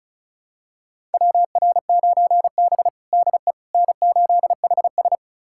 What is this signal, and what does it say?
Morse code, heard as a keyed tone, about 36 wpm, 700 Hz. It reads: WR9B DE N8HS
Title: USA Callsign DE Callsign - 4 Characters - 35wpm